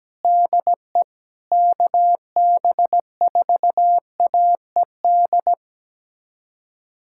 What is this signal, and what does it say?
Morse code, heard as a keyed tone, about 17 wpm, 700 Hz. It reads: DE KB4AED
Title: USA Callsign DE Callsign - 6 Characters - 17wpm